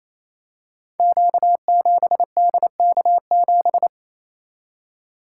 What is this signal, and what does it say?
Morse code, heard as a keyed tone, about 29 wpm, 700 Hz. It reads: Q7DK7
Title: Code Groups - 5 Letter:Numbers - 28wpm